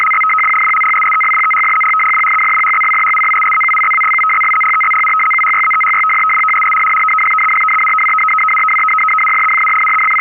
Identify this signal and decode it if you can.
Two-tone teleprinter signal, about 850 Hz shift.